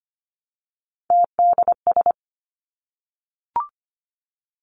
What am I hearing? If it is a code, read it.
Morse code, heard as a keyed tone, about 25 wpm, 700 Hz. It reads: TDS
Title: Three Challenging Letter-Number Combinations 25wpm